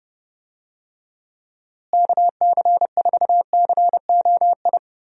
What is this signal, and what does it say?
Morse code, heard as a keyed tone, about 30 wpm, 700 Hz. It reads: KC4COI
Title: USA Call Signs - 6 Characters - Repeated 2x - 30wpm